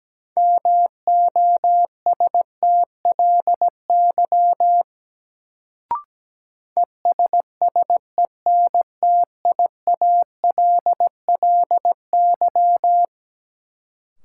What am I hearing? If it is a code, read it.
Morse code, heard as a keyed tone, about 17 wpm, 700 Hz. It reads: MOSTLY ESSENTIALLY
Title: Sentences from Top 1300 Words - New Word Drill - 17wpm